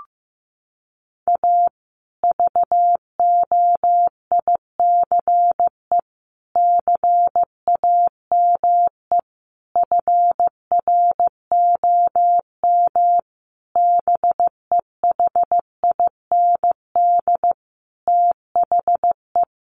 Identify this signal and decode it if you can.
Morse code, heard as a keyed tone, about 15 wpm, 700 Hz. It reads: A VOICE CAME FROM BEHIND THE
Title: Sentences from Top 1400 Words 15wpm